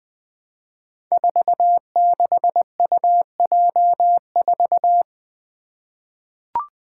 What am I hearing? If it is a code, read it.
Morse code, heard as a keyed tone, about 20 wpm, 700 Hz. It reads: 46UJ4
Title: Five Challenging Letter-Number Combinations 20wpm